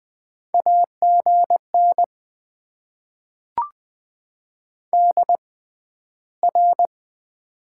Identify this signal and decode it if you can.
Morse code, heard as a keyed tone, about 20 wpm, spaced as for 12 wpm, 700 Hz. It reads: AGN DR
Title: Top 90 QSO Elements - 6x Character Spacing - 20wpm